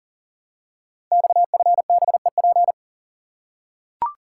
Morse code, heard as a keyed tone, about 41 wpm, 700 Hz. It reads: XFBEP
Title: Code Groups - 5 Letter:Numbers - 40wpm